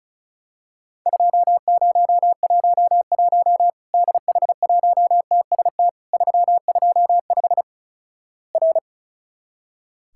Morse code, heard as a keyed tone, about 36 wpm, 700 Hz. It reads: 2011 DH1TST 325 R
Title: Worked All Europe DX Contest - QTC Exchange - 35wpm